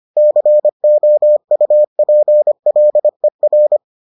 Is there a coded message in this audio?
Yes: Morse code from 0.2 to 3.8 s, heard as a keyed tone, about 25 wpm, 600 Hz. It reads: COUPLER